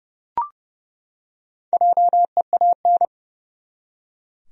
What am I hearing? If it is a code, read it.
Morse code, heard as a keyed tone, about 30 wpm, 700 Hz. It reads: JEAN
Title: USA Common Names - 30wpm